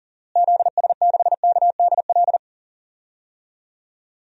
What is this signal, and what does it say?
Morse code, heard as a keyed tone, about 40 wpm, 700 Hz. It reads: ZS6KDL
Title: DXCC-100 Call Signs - 6 Characters - Repeated 2x - 40wpm